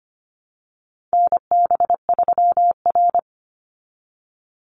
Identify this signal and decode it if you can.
Morse code, heard as a keyed tone, about 25 wpm, 700 Hz. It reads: NB3R
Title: Call Signs Repeated 3x - 25wpm